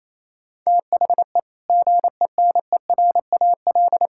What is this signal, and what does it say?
Morse code, heard as a keyed tone, about 28 wpm, 700 Hz. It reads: THE GENERAL
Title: Sentences from Top 1000 Words - Review Part 5 - 28wpm